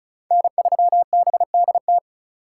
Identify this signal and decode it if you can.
Morse code, heard as a keyed tone, about 34 wpm, 700 Hz. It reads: N3BDT
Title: USA Call Signs - 3x Character Spacing - 35wpm